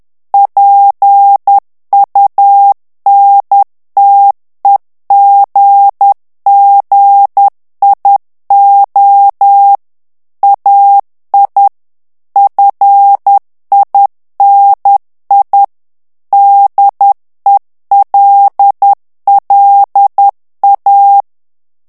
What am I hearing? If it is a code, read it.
Morse code, heard as a keyed tone, about 10 wpm, 800 Hz. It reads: PUNTEGGIO AI FINI DELLA
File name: ITALIEN texte clair 2 vitesse 10 mots minute